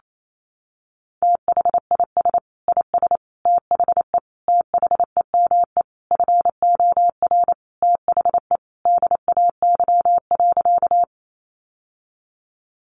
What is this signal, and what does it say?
Morse code, heard as a keyed tone, about 28 wpm, 700 Hz. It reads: THIS IS THE THEME FOR THE DAY.
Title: Sentences from Top 1100 Words 28wpm